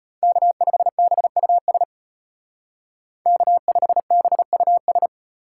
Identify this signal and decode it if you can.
Morse code, heard as a keyed tone, about 30 wpm, 700 Hz. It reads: K5BUS K5BUS
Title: Speed Racing - USA 1x3 Call Signs 25wpm